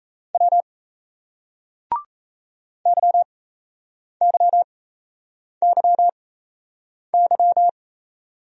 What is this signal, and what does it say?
Morse code, heard as a keyed tone, about 38 wpm, spaced as for 8 wpm, 700 Hz. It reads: W YYYY